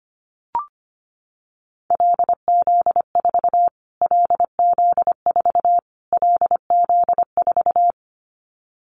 Morse code, heard as a keyed tone, about 25 wpm, 700 Hz. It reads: LZ4 LZ4 LZ4
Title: Callsign Prefixes for Top 25 DXCC - Repeated 3x - 25wpm